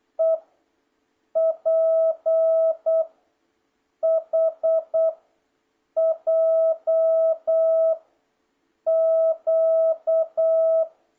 Morse code, heard as a keyed tone, about 8 wpm, 650 Hz. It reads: E P H J Q